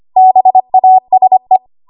Morse code, heard as a keyed tone, about 25 wpm, 750 Hz. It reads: BASE
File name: Archive / 25 WPM / 25WPM